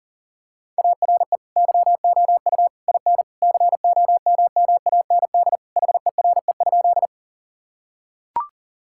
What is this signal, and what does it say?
Morse code, heard as a keyed tone, about 40 wpm, 700 Hz. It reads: ARE YOU IN COMMAND HERE?
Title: Sentences from Top 1600 Words 40wpm